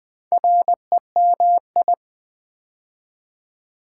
Morse code, heard as a keyed tone, about 20 wpm, 700 Hz. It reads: REMI
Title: Build A Word - 2 Letters At A Time - 6-Letter Words - 20wpm